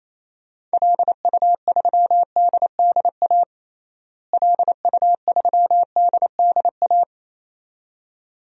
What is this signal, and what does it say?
Morse code, heard as a keyed tone, about 28 wpm, 700 Hz. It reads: LU3DDA LU3DDA
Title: DXCC-100 Call Signs - 6 Characters - Repeated 2x - 28wpm